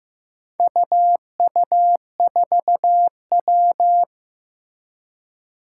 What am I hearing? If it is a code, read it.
Morse code, heard as a keyed tone, about 15 wpm, 700 Hz. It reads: UU4W
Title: Four Challenging Letter-Number Combinations 15wpm